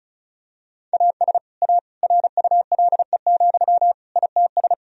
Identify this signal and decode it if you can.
Morse code, heard as a keyed tone, about 36 wpm, 700 Hz. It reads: AS A RULE, ITS
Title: Sentences from Top 1000 Words - Review Part 2 - 35wpm